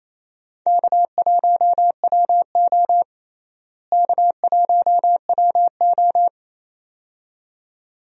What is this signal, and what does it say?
Morse code, heard as a keyed tone, about 28 wpm, 700 Hz. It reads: K1WO K1WO
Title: USA Call Signs - 4 Characters - Repeated 2x - 28wpm